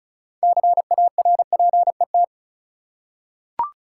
Morse code, heard as a keyed tone, about 34 wpm, 700 Hz. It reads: CARPET
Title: Six Letter Words - 35wpm